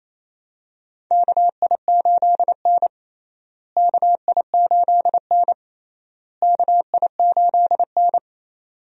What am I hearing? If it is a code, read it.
Morse code, heard as a keyed tone, about 28 wpm, 700 Hz. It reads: KI8N KI8N KI8N
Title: USA Call Signs - 4 Characters - Repeated 3x - 28wpm